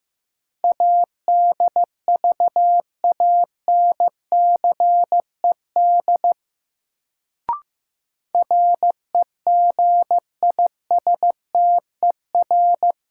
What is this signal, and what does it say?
Morse code, heard as a keyed tone, about 15 wpm, 700 Hz. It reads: ADVANCED REGISTER
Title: Sentences from Top 1200 Words - New Word Drill - 15wpm